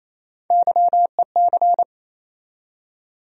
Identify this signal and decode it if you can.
Morse code, heard as a keyed tone, about 28 wpm, 700 Hz. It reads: YEC
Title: USA Call Sign Postfixes - 3 Letters 28wpm